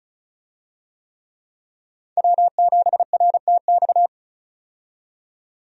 Morse code, heard as a keyed tone, about 34 wpm, 700 Hz. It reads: W7RTX
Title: USA Call Signs - 5 Characters - 35wpm